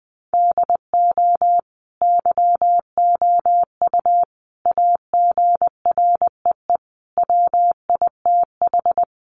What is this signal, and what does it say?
Morse code, heard as a keyed tone, about 20 wpm, 700 Hz. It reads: DO YOU AGREE WITH
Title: Sets of 5 Words Encore 20wpm